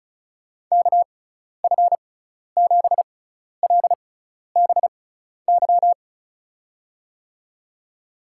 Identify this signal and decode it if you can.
Morse code, heard as a keyed tone, about 34 wpm, spaced as for 12 wpm, 690 Hz. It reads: KF7LBY
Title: USA Call Signs - 6 Characters - 6x Character Spacing - 35wpm